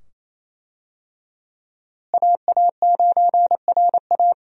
Morse code, heard as a keyed tone, about 28 wpm, 700 Hz. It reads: AA9RA
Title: USA Call Signs - 5 Characters - Repeated 2x - 28wpm